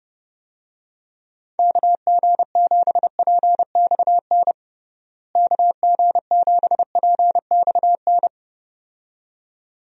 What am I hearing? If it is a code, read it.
Morse code, heard as a keyed tone, about 32 wpm, 700 Hz. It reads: KG7PXN KG7PXN
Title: USA Call Signs - 6 Characters - Repeated 2x - 30wpm